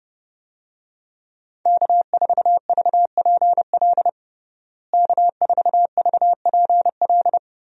Morse code, heard as a keyed tone, about 32 wpm, 700 Hz. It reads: K4VPL K4VPL
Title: USA Call Signs - 5 Characters - Repeated 3x - 30wpm